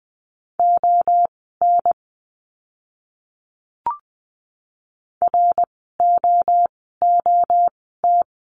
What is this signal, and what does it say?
Morse code, heard as a keyed tone, about 20 wpm, spaced as for 15 wpm, 700 Hz. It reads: ON ROOT
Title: 068 Lesson 12f - Letter d - Words 20wpm